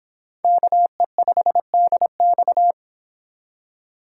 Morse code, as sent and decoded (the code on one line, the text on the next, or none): -.- . ..... -.. -..-
KE5DX